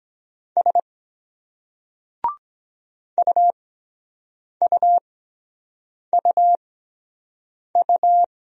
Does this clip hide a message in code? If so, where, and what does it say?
Yes: Morse code from 0.6 to 8.2 s, heard as a keyed tone, about 23 wpm, spaced as for 7 wpm, 700 Hz. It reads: S UUUU